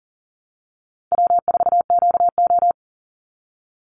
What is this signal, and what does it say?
Morse code, heard as a keyed tone, about 40 wpm, 700 Hz. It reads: W4QO
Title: Call Signs Repeated 3x - 40wpm